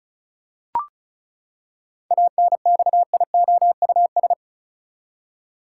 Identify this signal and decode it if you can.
Morse code, heard as a keyed tone, about 34 wpm, 700 Hz. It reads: ANXIOUS
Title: Seven Letter Words - 35wpm